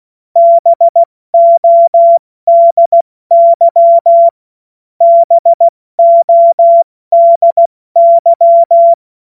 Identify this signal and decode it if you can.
Morse code, heard as a keyed tone, about 16 wpm, spaced as for 14 wpm, 680 Hz. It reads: BODY BODY
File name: Morsekurs / Witze / Witz 23 Januar 2024 70 F80